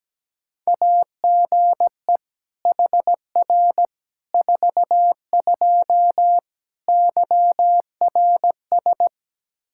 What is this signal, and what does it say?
Morse code, heard as a keyed tone, about 17 wpm, 700 Hz. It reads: AGE HR 42 YRS
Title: Practice QSO - Exchange 3 - More Info - 17wpm